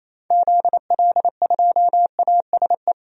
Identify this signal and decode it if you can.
Morse code, heard as a keyed tone, about 28 wpm, 700 Hz. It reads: ZL2ASE
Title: DXCC-100 Call Signs - 6 Characters - 28wpm